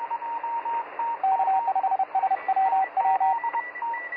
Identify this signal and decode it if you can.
Morse code, heard as a keyed tone, about 31 wpm, 770 Hz, with radio noise. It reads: K5SWW
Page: I was tunning around 7240 khz tonight to try my hand at copying some morse code from the ether. Well, I experienced what a pile sounds like - for real